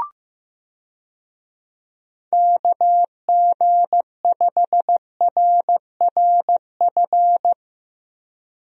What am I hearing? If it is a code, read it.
Morse code, heard as a keyed tone, about 15 wpm, 700 Hz. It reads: KG5RRF
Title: USA Call Signs - 6 Characters - Repeated 2x - 15wpm